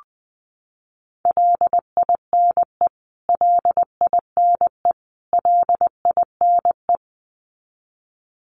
Morse code, heard as a keyed tone, about 20 wpm, 700 Hz. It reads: LINE LINE LINE
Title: Top 500 Words Repeated 20wpm